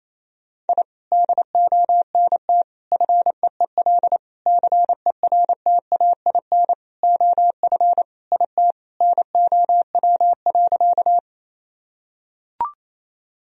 Morse code, heard as a keyed tone, about 28 wpm, 700 Hz. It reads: I DONT FEEL CERTAIN OF IT NOW.
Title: Sentences from Top 1000 Words - Review Part 2 - 28wpm